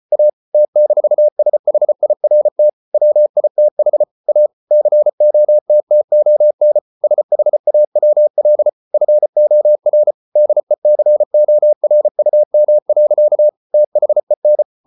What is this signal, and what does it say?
Morse code, heard as a keyed tone, about 34 wpm, 600 Hz. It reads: A T-SHIRT WITH A COTTON SHAWL FOR DECORUM. THEN